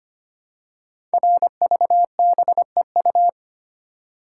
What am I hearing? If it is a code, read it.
Morse code, heard as a keyed tone, about 25 wpm, 700 Hz. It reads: RVBEU
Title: Code Groups - 5 Letters - 25wpm